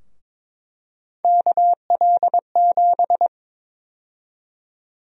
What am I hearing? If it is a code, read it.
Morse code, heard as a keyed tone, about 22 wpm, 700 Hz. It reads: KL7
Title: USA Call Sign Prefixes - Repeated 2x - 22wpm